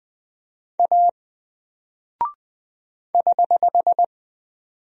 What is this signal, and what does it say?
Morse code, heard as a keyed tone, about 20 wpm, 700 Hz. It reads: A <HH>
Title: Single Character (Letter:Number:Punctuation:Prosign) 20wpm